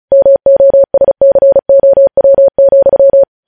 Morse code, heard as a keyed tone, about 36 wpm, 550 Hz. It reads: MOSCOW,
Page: News headlines in Morse code at 35 wpm